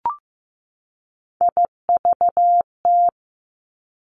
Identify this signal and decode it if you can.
Morse code, heard as a keyed tone, about 15 wpm, 700 Hz. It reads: IVT
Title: Three Challenging Letter-Number Combinations 15wpm